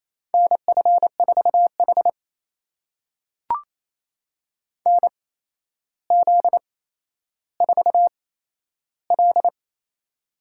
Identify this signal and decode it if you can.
Morse code, heard as a keyed tone, about 28 wpm, spaced as for 12 wpm, 700 Hz. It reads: NF4H NZ4L
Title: USA Call Signs - 4 Characters - 8x Character Spacing - 28wpm